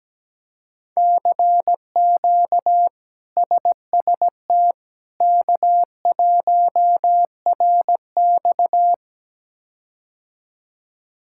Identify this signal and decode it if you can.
Morse code, heard as a keyed tone, about 17 wpm, 700 Hz. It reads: CQ SST K1RX
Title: K1USN SST QSO Practice - 17wpm